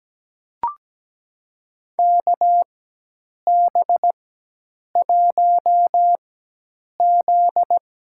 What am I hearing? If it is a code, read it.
Morse code, heard as a keyed tone, about 17 wpm, spaced as for 8 wpm, 700 Hz. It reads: KB1Z
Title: USA Call Signs - 4x Character Spacing - 17wpm